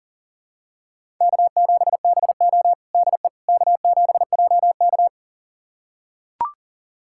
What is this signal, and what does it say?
Morse code, heard as a keyed tone, about 40 wpm, 700 Hz. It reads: K7BO DE K7JK
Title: USA Callsign DE Callsign - 4 Characters - 40wpm